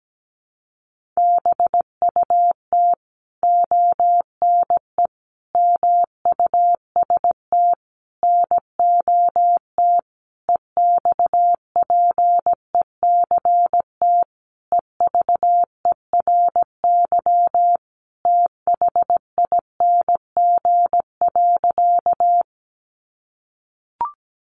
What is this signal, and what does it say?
Morse code, heard as a keyed tone, about 17 wpm, 700 Hz. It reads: BUT ONE MUST NOT EXPECT EVERYTHING.
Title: Sentences from Top 1100 Words 17wpm